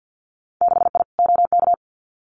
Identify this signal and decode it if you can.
Morse code, heard as a keyed tone, about 50 wpm, 700 Hz. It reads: 6I OK
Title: Winter Field Day Exchanges Encore 50wpm